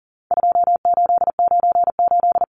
Morse code, heard as a keyed tone, about 40 wpm, 700 Hz. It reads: 2898
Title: Numbers - 4-Digit 40wpm